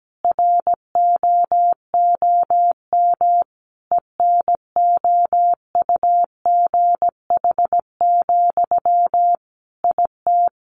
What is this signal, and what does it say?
Morse code, heard as a keyed tone, about 17 wpm, 700 Hz. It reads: ROOM ENOUGH, IT
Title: Sentences from Top 1400 Words 17wpm